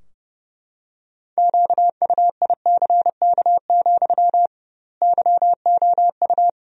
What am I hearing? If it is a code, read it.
Morse code, heard as a keyed tone, about 32 wpm, 700 Hz. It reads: QUICK, YOU
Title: 2x Word-Spacing - Sentences from Top 500 Words - 30wpm